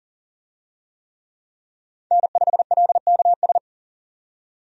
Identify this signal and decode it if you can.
Morse code, heard as a keyed tone, about 40 wpm, 700 Hz. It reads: N5LKS
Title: USA Call Signs - 5 Characters - 40wpm